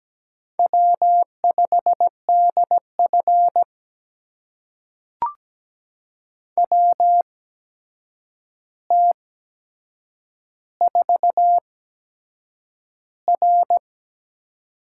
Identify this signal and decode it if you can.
Morse code, heard as a keyed tone, about 17 wpm, spaced as for 7 wpm, 700 Hz. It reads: W5DF WT4R
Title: USA Call Signs - 4 Characters - 8x Character Spacing - 17wpm